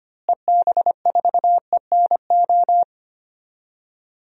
Morse code, heard as a keyed tone, about 25 wpm, 700 Hz. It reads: EB4ENO
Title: DXCC-100 Call Signs - 6 Characters - Repeated 2x - 25wpm